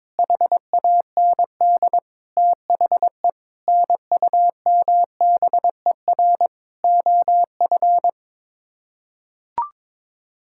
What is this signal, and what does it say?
Morse code, heard as a keyed tone, about 22 wpm, 700 Hz. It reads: HAND THE NUMBER OF
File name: Sets of 9 Words - 22wpm